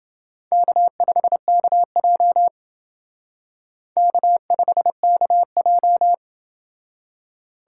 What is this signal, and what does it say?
Morse code, heard as a keyed tone, about 28 wpm, 700 Hz. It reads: K5KJ K5KJ